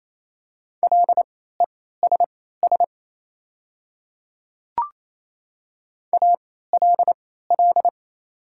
Morse code, heard as a keyed tone, about 28 wpm, spaced as for 16 wpm, 700 Hz. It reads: LESS ALL
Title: 3x Character Spacing - Top 500 Words - 28wpm